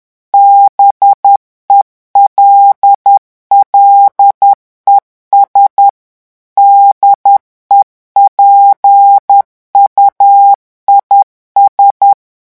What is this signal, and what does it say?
Morse code, heard as a keyed tone, about 10 wpm, 800 Hz. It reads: BELLES DEPUIS
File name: morse mp3 / revision9 citations humoristiques vitesse 10 mots minute